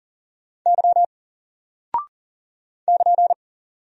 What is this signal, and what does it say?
Morse code, heard as a keyed tone, about 41 wpm, 700 Hz. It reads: Y <KN>
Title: Single Character (Letter:Number:Punctuation:Prosign) 40wpm